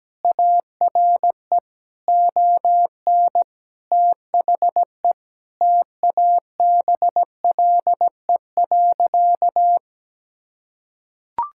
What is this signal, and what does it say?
Morse code, heard as a keyed tone, about 17 wpm, 700 Hz. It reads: ARE ON THE TABLE.
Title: Sentences from Top 1000 Words - Review Part 6 - 17wpm